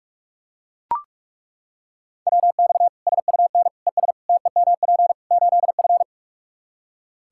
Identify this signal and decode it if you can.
Morse code, heard as a keyed tone, about 41 wpm, 700 Hz. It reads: WX SUN ES TEMP 8F
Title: Practice QSO - Exchange 2 - Info - 45wpm